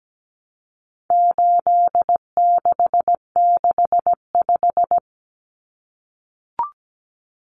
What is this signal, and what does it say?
Morse code, heard as a keyed tone, about 17 wpm, 700 Hz. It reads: 8665
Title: Numbers - 4-Digit 17wpm